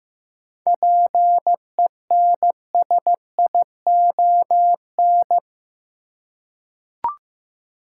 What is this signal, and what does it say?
Morse code, heard as a keyed tone, about 15 wpm, 700 Hz. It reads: PENSION
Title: Seven Letter Words - 15wpm